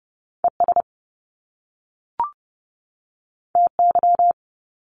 Morse code, heard as a keyed tone, about 30 wpm, 700 Hz. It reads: ES TY
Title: Top Two Letter Combinations in Words 30wpm